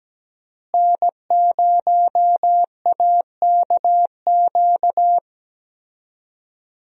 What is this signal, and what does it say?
Morse code, heard as a keyed tone, about 17 wpm, 700 Hz. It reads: N0AKQ
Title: USA Call Signs - 5 Characters - Repeated 3x - 17wpm